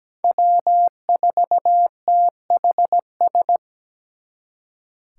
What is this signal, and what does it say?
Morse code, heard as a keyed tone, about 20 wpm, 700 Hz. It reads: W4THS
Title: Speed Racing - USA 1x3 Call Signs 17wpm